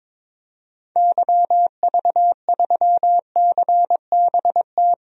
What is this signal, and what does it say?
Morse code, heard as a keyed tone, about 21 wpm, 700 Hz. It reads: YV3CBT
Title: DXCC-100 Call Signs - 6 Characters - Repeated 3x - 22wpm